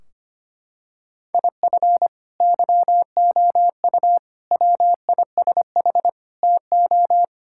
Morse code, heard as a keyed tone, about 25 wpm, 700 Hz. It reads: IF YOU WISH TO
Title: Sentences from Top 1600 Words 25wpm